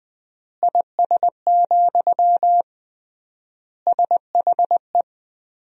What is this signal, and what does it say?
Morse code, heard as a keyed tone, about 21 wpm, 700 Hz. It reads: IS, SHE
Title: 3x Word-Spacing - Sentences from Top 100 Words - 20wpm